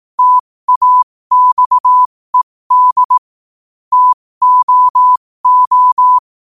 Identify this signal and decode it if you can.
Morse code, heard as a keyed tone, about 18 wpm, spaced as for 15 wpm, 1000 Hz. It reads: TAXED TOO
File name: qotdfiles / 15wpm